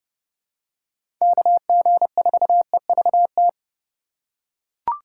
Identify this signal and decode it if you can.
Morse code, heard as a keyed tone, about 32 wpm, 700 Hz. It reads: KG4EVT
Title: USA Call Signs - 6 Characters - 3x Character Spacing - 30wpm